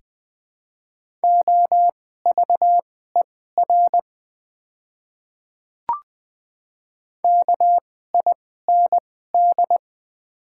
Morse code, heard as a keyed tone, about 20 wpm, spaced as for 14 wpm, 700 Hz. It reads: OVER KIND
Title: 2x Character Spacing - Top 200 Words - 20wpm